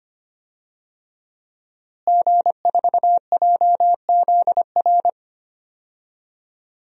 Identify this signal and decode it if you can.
Morse code, heard as a keyed tone, about 25 wpm, 700 Hz. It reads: G4JZR
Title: DXCC-100 Call Signs - 5 Characters - Repeated 3x - 25wpm